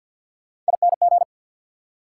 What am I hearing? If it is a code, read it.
Morse code, heard as a keyed tone, about 67 wpm, spaced as for 58 wpm, 690 Hz. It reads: ING
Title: Speed-Racing Prefixes and Suffixes 50wpm